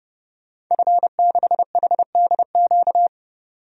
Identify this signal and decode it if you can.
Morse code, heard as a keyed tone, about 32 wpm, 700 Hz. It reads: F6HDQ
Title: DXCC-100 Call Signs - 5 Characters - 30wpm